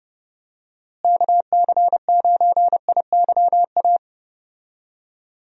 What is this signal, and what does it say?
Morse code, heard as a keyed tone, about 32 wpm, 700 Hz. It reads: KC9IYA
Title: USA Call Signs - 6 Characters - Repeated 3x - 30wpm